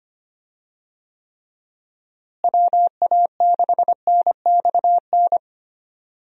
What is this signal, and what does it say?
Morse code, heard as a keyed tone, about 25 wpm, 700 Hz. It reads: WA6NXN
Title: USA Call Signs - 6 Characters - Repeated 2x - 25wpm